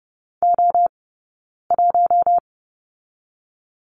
Morse code, heard as a keyed tone, about 30 wpm, 700 Hz. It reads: O 1
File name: course / 30:10wpm / 086 Lesson 15d - Number 5 - Two Characters 30wpm